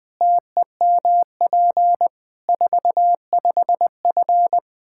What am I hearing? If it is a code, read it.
Morse code, heard as a keyed tone, about 20 wpm, 700 Hz. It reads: TEMP 45F
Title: Practice QSO - Exchange 2 - Info - 20wpm